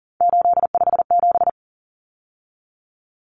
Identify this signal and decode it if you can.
Morse code, heard as a keyed tone, about 40 wpm, 700 Hz. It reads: 857
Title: Numbers - 3-Digit 40wpm